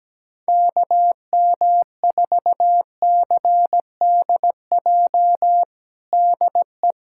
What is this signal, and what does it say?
Morse code, heard as a keyed tone, about 17 wpm, 700 Hz. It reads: KM4CDJ DE
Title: USA Callsign DE Callsign - 6 Characters - 17wpm